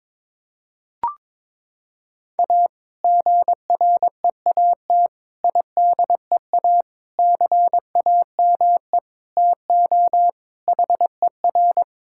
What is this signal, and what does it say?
Morse code, heard as a keyed tone, about 22 wpm, 700 Hz. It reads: A GREAT IDEA CAME TO HER
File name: practice / 22wpm / Sentences with 2-Word Phrases - 22wpm